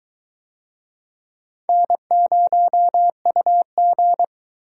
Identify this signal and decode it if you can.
Morse code, heard as a keyed tone, about 21 wpm, 700 Hz. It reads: N0UG